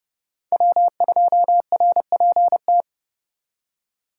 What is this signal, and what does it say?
Morse code, heard as a keyed tone, about 37 wpm, 700 Hz. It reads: W2RPT
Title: Speed Racing - USA 1x3 Call Signs 30wpm